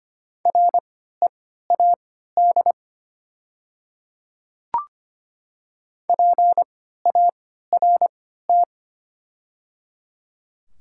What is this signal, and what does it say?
Morse code, heard as a keyed tone, about 25 wpm, spaced as for 14 wpm, 700 Hz. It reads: READ PART
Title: 3x Character Spacing - Top 200 Words - 25wpm